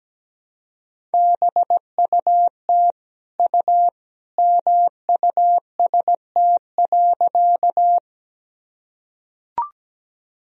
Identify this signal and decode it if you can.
Morse code, heard as a keyed tone, about 17 wpm, 700 Hz. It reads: BUT U MUST.
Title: Sentences from Top 200 Words and Abbreviations 17wpm